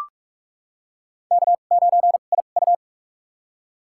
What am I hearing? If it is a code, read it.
Morse code, heard as a keyed tone, about 44 wpm, spaced as for 32 wpm, 700 Hz. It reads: K9IU
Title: USA Call Signs - 4 Characters - 2x Character Spacing - 45wpm